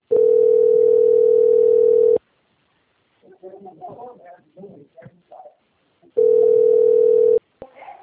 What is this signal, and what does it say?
Ringback tone (North America), 440 + 480 Hz, with speech.
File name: out- - - - - .25497